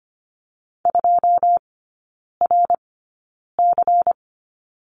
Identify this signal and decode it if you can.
Morse code, heard as a keyed tone, about 25 wpm, spaced as for 9 wpm, 700 Hz. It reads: 2RC
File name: Lesson 17e - Letter u - Three Characters 25wpm